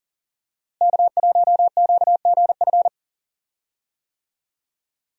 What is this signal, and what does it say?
Morse code, heard as a keyed tone, about 40 wpm, 700 Hz. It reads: K1QGF
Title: USA Call Signs - 5 Characters - Repeated 3x - 40wpm